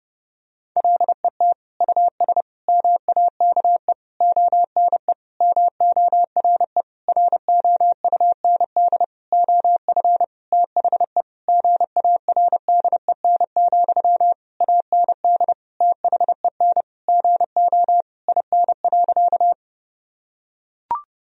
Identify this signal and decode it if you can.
Morse code, heard as a keyed tone, about 30 wpm, 700 Hz. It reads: LET US MAKE ONE MORE ROUND OF THE GARDEN, AND THEN GO IN.
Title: Sentences from Top 1000 Words - Review Part 4 - 30wpm